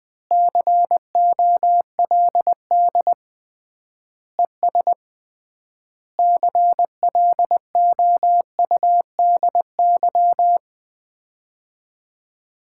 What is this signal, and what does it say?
Morse code, heard as a keyed tone, about 20 wpm, 700 Hz. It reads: COLD ES CLOUDY
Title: Common QSO Phrases - 3x Word-Spacing - 20wpm